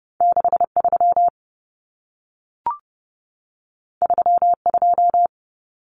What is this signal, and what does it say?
Morse code, heard as a keyed tone, about 30 wpm, 700 Hz. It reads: 63 32
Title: Common Cut Numbers - 2-Digit 30wpm